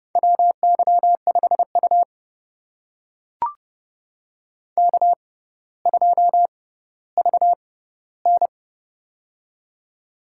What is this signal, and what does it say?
Morse code, heard as a keyed tone, about 30 wpm, spaced as for 15 wpm, 700 Hz. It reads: WY5U K2VN
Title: USA Call Signs - 4 Characters - 6x Character Spacing - 30wpm